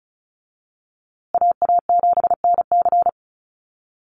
Morse code, heard as a keyed tone, about 34 wpm, 700 Hz. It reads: AA7NC
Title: USA Call Signs Repeated 3x 35wpm